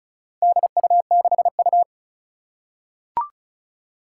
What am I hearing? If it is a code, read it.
Morse code, heard as a keyed tone, about 36 wpm, 700 Hz. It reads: DU6U
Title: Four Challenging Letter-Number Combinations 35wpm